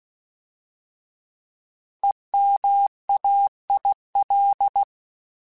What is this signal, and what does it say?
Morse code, heard as a keyed tone, about 16 wpm, 800 Hz. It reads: EMAIL
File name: Alphabetical / Alphabetical mp3s / Es @15